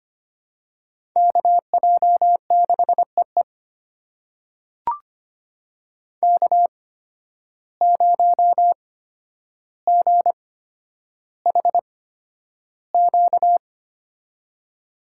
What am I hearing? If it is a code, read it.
Morse code, heard as a keyed tone, about 25 wpm, spaced as for 11 wpm, 700 Hz. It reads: KJ6EE K0GHQ
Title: USA Call Signs - 5 Characters - 8x Character Spacing - 25wpm